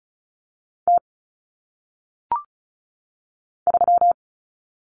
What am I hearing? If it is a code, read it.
Morse code, heard as a keyed tone, about 34 wpm, 700 Hz. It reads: T 3
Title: Common Cut Numbers - 1-Digit 35wpm